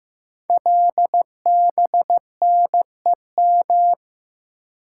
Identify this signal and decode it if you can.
Morse code, heard as a keyed tone, about 15 wpm, 700 Hz. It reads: LBNEM
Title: Code Groups - 5 Letters - 15wpm